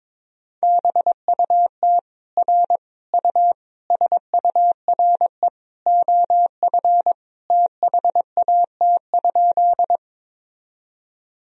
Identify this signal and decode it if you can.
Morse code, heard as a keyed tone, about 22 wpm, 700 Hz. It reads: BUT R U SURE OF THAT?
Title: Sentences from Top 300 Words and Abbreviations 22wpm